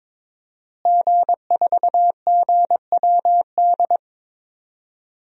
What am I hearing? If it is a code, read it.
Morse code, heard as a keyed tone, about 22 wpm, 700 Hz. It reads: G4GWD
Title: DXCC-100 Call Signs - 5 Characters - Repeated 2x - 22wpm